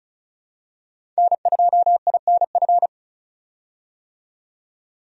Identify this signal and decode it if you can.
Morse code, heard as a keyed tone, about 36 wpm, 700 Hz. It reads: N2INF
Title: USA Call Signs - 5 Characters - 35wpm